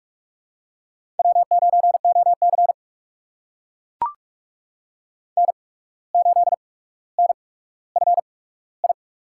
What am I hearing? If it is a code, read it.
Morse code, heard as a keyed tone, about 44 wpm, spaced as for 18 wpm, 700 Hz. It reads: W9OC N8NFI
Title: USA Call Signs - 8x Character Spacing - 45wpm